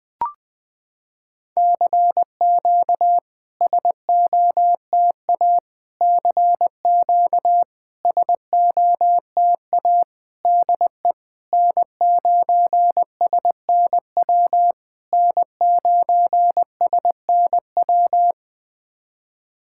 Morse code, heard as a keyed tone, about 21 wpm, 700 Hz. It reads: CQ SOTA CQ SOTA DE N9SNW N9SNW
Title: SOTA QSO Practice 20wpm